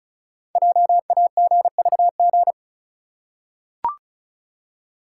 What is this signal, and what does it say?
Morse code, heard as a keyed tone, about 36 wpm, 700 Hz. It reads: JAGVG
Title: Code Groups - 5 Letters - 35wpm